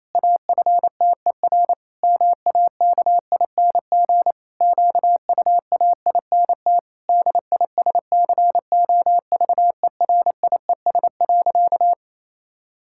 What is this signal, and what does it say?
Morse code, heard as a keyed tone, about 27 wpm, 700 Hz. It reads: AFTER MAKING QUAINT DISCOVERIES.
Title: Pangram Challenge v2 - 28wpm